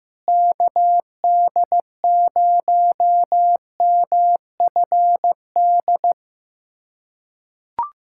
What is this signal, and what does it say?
Morse code, heard as a keyed tone, about 15 wpm, 700 Hz. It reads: KD0MFD
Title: USA Callsign DE Callsign 15wpm